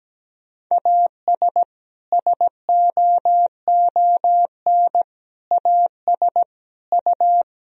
Morse code, heard as a keyed tone, about 17 wpm, 700 Hz. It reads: AS SOON AS U
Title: Sentences from Top 400 Words and Abbreviations 17wpm